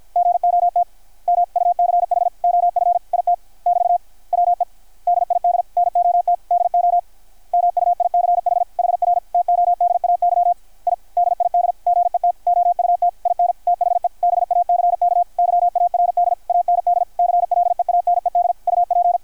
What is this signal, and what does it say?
Morse code, heard as a keyed tone, about 51 wpm, 720 Hz, with radio noise. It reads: GOT MUCH OF IT <BT> WE DID NOT DO MUICH HR TODAY I DID GET OUT IN THE BACK YARD AND CLEANED UP
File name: archived / CRSnet